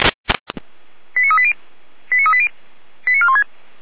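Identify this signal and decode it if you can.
Five-tone selective call (ZVEI), heard as quick runs of five tones, tones of about 70 ms. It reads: 89380 89380 89416
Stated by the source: ZVEI five-tone selective call received on an SDR.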